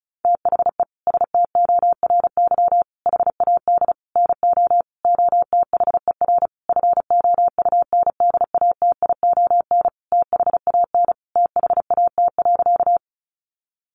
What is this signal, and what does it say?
Morse code, heard as a keyed tone, about 34 wpm, 700 Hz. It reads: THE STORY HAD NO OTHER FOUNDATION THAN THAT.
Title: Sentences from Top 1200 Words 35wpm